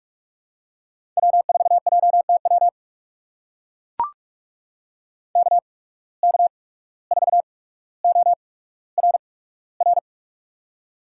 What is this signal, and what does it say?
Morse code, heard as a keyed tone, about 44 wpm, spaced as for 18 wpm, 700 Hz. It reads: W4JTW KK4ORR
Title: USA Call Signs - 8x Character Spacing - 45wpm